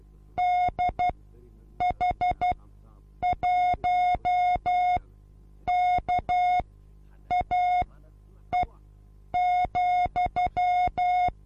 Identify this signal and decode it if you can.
Morse code, heard as a keyed tone, about 12 wpm, spaced as for 8 wpm, 720 Hz. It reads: DH1KAE,
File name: Onlinelehrgang / morsen / DJ4UF Morsen23